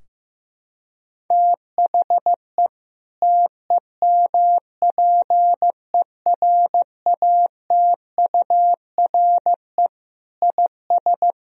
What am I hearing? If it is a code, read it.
Morse code, heard as a keyed tone, about 15 wpm, 700 Hz. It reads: THE TEMPERATURE IS
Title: Sentences from Top 1000 Words - Review Part 4 - 15wpm